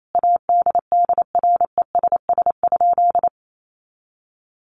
Morse code, heard as a keyed tone, about 28 wpm, 700 Hz. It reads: ADDRESS?
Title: Sentences from Top 1000 Words 28wpm